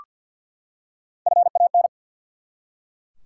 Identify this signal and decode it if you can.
Morse code, heard as a keyed tone, about 50 wpm, 700 Hz. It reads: FAN
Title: Three Letter Words - 50wpm